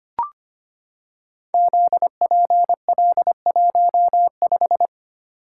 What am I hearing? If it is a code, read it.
Morse code, heard as a keyed tone, about 25 wpm, 700 Hz. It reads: ZPL15
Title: Code Groups - 5 Letter:Numbers - 25wpm